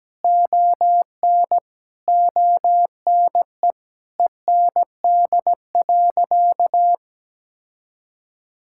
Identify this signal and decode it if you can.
Morse code, heard as a keyed tone, about 17 wpm, 700 Hz. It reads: ON ONE END.
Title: Sentences from Top 1600 Words 17wpm